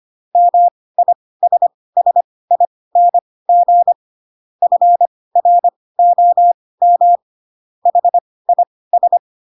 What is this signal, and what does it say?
Morse code, heard as a keyed tone, about 25 wpm, spaced as for 18 wpm, 700 Hz. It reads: MISSING FROM HIS
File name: Pinocchio / Pinocchio ch2 12 25 ik1hge